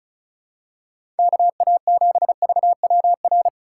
Morse code, heard as a keyed tone, about 34 wpm, 700 Hz. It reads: KA7VWR
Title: USA Call Signs - 6 Characters - 2x Character Spacing - 35wpm